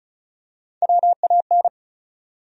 Morse code, heard as a keyed tone, about 34 wpm, 700 Hz. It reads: WAN
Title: USA Call Sign Postfixes - 3 Letters 35wpm